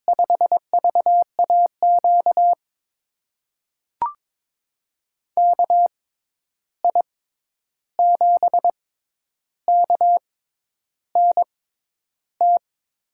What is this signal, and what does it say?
Morse code, heard as a keyed tone, about 22 wpm, spaced as for 10 wpm, 700 Hz. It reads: 5VAQ KI7KNT
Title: USA Call Signs - 6 Characters - 6x Character Spacing - 22wpm